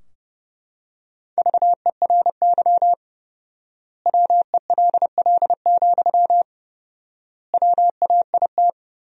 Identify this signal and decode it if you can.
Morse code, heard as a keyed tone, about 30 wpm, 700 Hz. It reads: VERY WELL, WAIT
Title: 4x Word-Spacing - Sentences from Top 500 Words - 30wpm